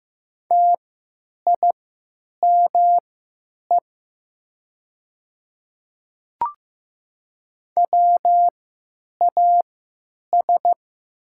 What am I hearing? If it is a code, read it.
Morse code, heard as a keyed tone, about 15 wpm, spaced as for 9 wpm, 700 Hz. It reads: TIME WAS
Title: 3x Character Spacing - Top 200 Words - 15wpm